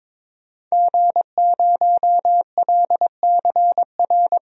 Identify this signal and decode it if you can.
Morse code, heard as a keyed tone, about 21 wpm, 700 Hz. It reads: G0LCR
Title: DXCC-100 Call Signs - 5 Characters - Repeated 2x - 22wpm